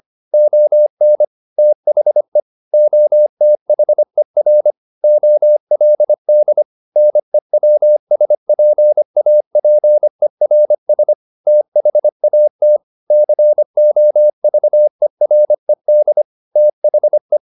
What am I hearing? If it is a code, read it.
Morse code, heard as a keyed tone, about 25 wpm, 600 Hz. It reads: ON THE OTHER OLD NEWSPAPERS THAT COVERED THE
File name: the old man and the sea part 1 25wpm